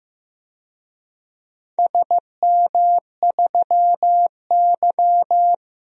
Morse code, heard as a keyed tone, about 15 wpm, 700 Hz. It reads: SM3Y
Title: DXCC-100 Call Signs - 4 Characters - Repeated 2x - 15wpm